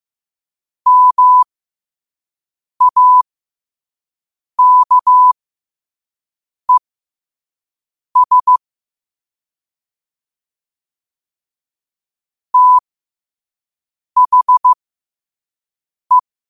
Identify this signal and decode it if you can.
Morse code, heard as a keyed tone, about 15 wpm, spaced as for 5 wpm, 1000 Hz. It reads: MAKES THE
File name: 5wpm